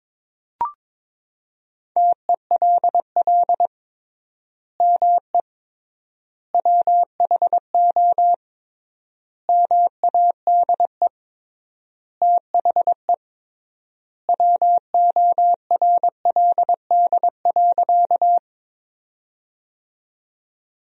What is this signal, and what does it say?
Morse code, heard as a keyed tone, about 22 wpm, 700 Hz. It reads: TELL ME WHO MADE THE WORLD.
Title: 3x Word-Spacing - Sentences from Top 200 Words - 22wpm